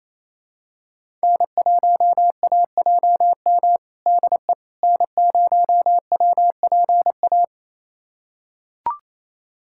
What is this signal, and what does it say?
Morse code, heard as a keyed tone, about 27 wpm, 700 Hz. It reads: N1AJM DE N0WPA
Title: USA Callsign DE Callsign 28wpm